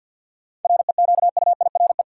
Morse code, heard as a keyed tone, about 53 wpm, 700 Hz. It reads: REQUIRE
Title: Sentences from Top 900 Words - New Word Drill - 50wpm